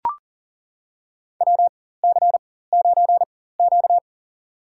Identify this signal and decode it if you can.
Morse code, heard as a keyed tone, about 40 wpm, spaced as for 19 wpm, 700 Hz. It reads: WC9Q
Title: USA Call Signs - 4 Characters - 4x Character Spacing - 40wpm